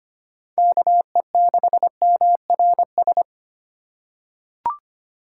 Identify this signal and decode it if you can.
Morse code, heard as a keyed tone, about 25 wpm, 700 Hz. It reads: KE6MRS
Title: USA Call Signs - 6 Characters - 8x Character Spacing - 25wpm